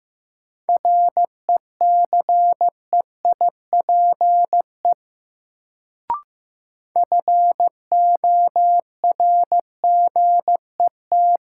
Morse code, heard as a keyed tone, about 15 wpm, 700 Hz. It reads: RECEIPE FORGET
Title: Sentences from Top 1200 Words - New Word Drill - 15wpm